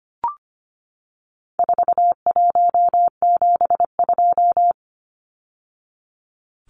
Morse code, heard as a keyed tone, about 25 wpm, 700 Hz. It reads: 4172
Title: Numbers - 4-Digit 25wpm